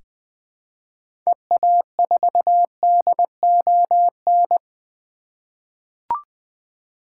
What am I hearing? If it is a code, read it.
Morse code, heard as a keyed tone, about 20 wpm, 700 Hz. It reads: EA4DON
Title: 2023 SOTA Activator Call Signs - 20wpm